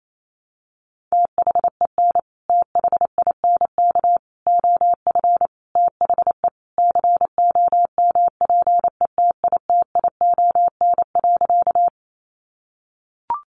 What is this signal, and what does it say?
Morse code, heard as a keyed tone, about 28 wpm, 700 Hz. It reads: THEN THINK OF THE COMPETITION.
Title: Sentences from Top 1000 Words 28wpm